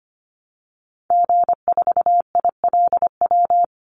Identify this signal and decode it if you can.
Morse code, heard as a keyed tone, about 25 wpm, 700 Hz. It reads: G4ILW
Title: Call Signs Repeated 2x - 25wpm